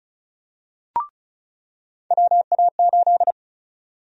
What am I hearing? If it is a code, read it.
Morse code, heard as a keyed tone, about 34 wpm, 700 Hz. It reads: WA8
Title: USA Call Sign Prefixes - 35wpm